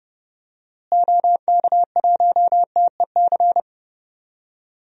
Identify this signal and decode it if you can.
Morse code, heard as a keyed tone, about 30 wpm, 700 Hz. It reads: OK1TEC
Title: DXCC-100 Call Signs - 6 Characters - Repeated 3x - 30wpm